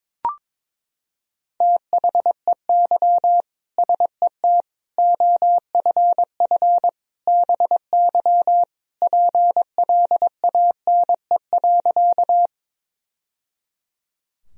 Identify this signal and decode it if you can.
Morse code, heard as a keyed tone, about 22 wpm, 700 Hz. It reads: THEY SET OFF BY PLANE.
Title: Sentences from Top 1000 Words - Review Part 3 - 22wpm